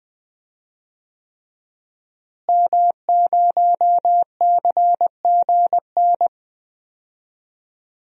Morse code, heard as a keyed tone, about 21 wpm, 700 Hz. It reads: M0CGN
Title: DXCC-100 Call Signs - 5 Characters - Repeated 3x - 20wpm